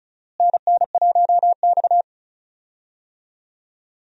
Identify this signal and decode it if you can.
Morse code, heard as a keyed tone, about 36 wpm, 700 Hz. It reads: NN1X
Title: USA Call Signs - 4 Characters - Repeated 2x - 35wpm